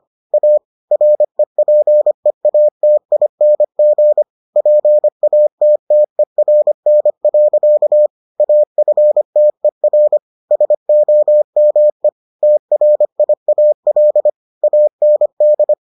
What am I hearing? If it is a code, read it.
Morse code, heard as a keyed tone, about 25 wpm, 600 Hz. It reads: A REPEATING PATTERN. AFTER SOME TRIAL AND